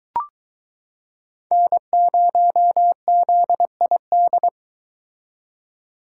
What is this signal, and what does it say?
Morse code, heard as a keyed tone, about 23 wpm, 700 Hz. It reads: N0ZID